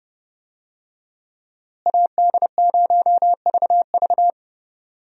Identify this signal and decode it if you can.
Morse code, heard as a keyed tone, about 30 wpm, 700 Hz. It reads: AD0VV
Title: USA Call Signs - 5 Characters - Repeated 2x - 30wpm